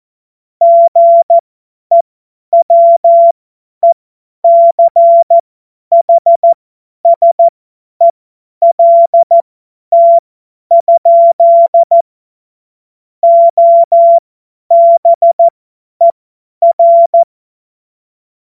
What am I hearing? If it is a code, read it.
Morse code, heard as a keyed tone, about 14 wpm, spaced as for 10 wpm, 680 Hz. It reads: GEWECHSELT? OBER
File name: Morsekurs / Witze / Witz 25. Mai 2021, 50 F70